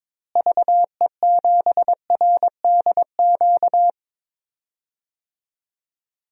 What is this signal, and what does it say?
Morse code, heard as a keyed tone, about 22 wpm, 700 Hz. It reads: VE7RDQ
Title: DXCC-100 Call Signs - 6 Characters - Repeated 3x - 22wpm